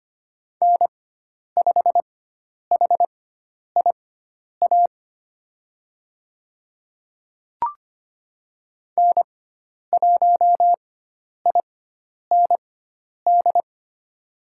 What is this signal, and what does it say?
Morse code, heard as a keyed tone, about 25 wpm, spaced as for 10 wpm, 700 Hz. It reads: N5HIA N1IND
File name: Lesson 16g - Letter c - Call Signs 25wpm